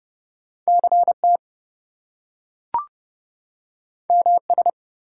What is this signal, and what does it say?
Morse code, heard as a keyed tone, about 30 wpm, 700 Hz. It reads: CT MS
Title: US States and Canadian Provinces 30wpm